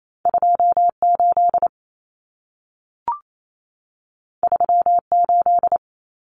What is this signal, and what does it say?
Morse code, heard as a keyed tone, about 28 wpm, 700 Hz. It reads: 28 38
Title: Common Cut Numbers - 2-Digit 28wpm